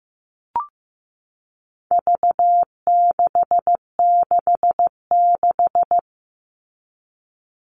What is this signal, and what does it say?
Morse code, heard as a keyed tone, about 15 wpm, 700 Hz. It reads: V666
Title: All Cut Numbers - 4-Digit 15wpm